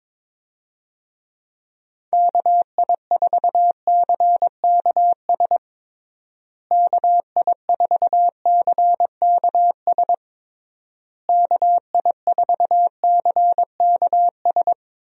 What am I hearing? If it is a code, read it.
Morse code, heard as a keyed tone, about 22 wpm, 700 Hz. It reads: KI4CKS KI4CKS KI4CKS
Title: USA Call Signs - 6 Characters - Repeated 3x - 22wpm